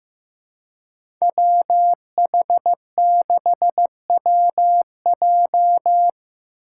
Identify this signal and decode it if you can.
Morse code, heard as a keyed tone, about 15 wpm, 700 Hz. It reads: WH6WJ
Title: USA Call Signs - 5 Characters - 6x Character Spacing - 15wpm